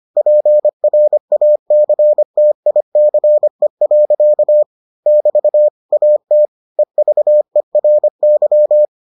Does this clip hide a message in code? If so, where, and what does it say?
Yes: Morse code from 0.2 to 8.8 s, heard as a keyed tone, about 25 wpm, 600 Hz. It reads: PRACTICE. <BT> AT EVERY